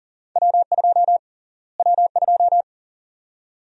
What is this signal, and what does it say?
Morse code, heard as a keyed tone, about 40 wpm, 700 Hz. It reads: W2 W2
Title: USA Call Sign Prefixes - Repeated 2x - 40wpm